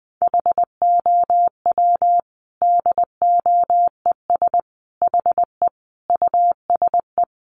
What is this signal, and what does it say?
Morse code, heard as a keyed tone, about 20 wpm, 700 Hz. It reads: HOW DOES HE USE
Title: Sentences from Top 200 Words 20wpm